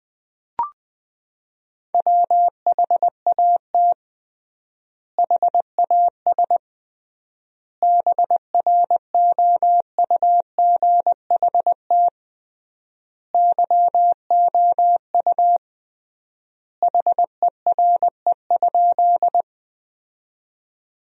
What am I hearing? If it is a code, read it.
Morse code, heard as a keyed tone, about 20 wpm, 700 Hz. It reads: WHAT HAS BROUGHT YOU HERE?
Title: 3x Word-Spacing - Sentences from Top 500 Words - 20wpm